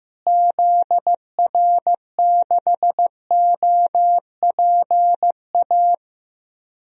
Morse code, heard as a keyed tone, about 15 wpm, 700 Hz. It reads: ZR6OPA
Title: DXCC-100 Call Signs - 6 Characters - Repeated 2x - 15wpm